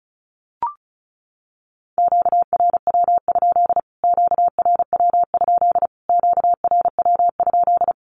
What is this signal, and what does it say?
Morse code, heard as a keyed tone, about 34 wpm, 700 Hz. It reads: QRW? QRW? QRW?
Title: Q-Codes Repeated 3x 35wpm